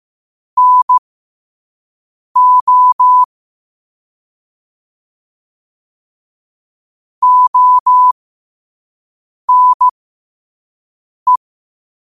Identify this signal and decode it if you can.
Morse code, heard as a keyed tone, about 15 wpm, spaced as for 5 wpm, 1000 Hz. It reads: NO ONE
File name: qotdfiles / 5wpm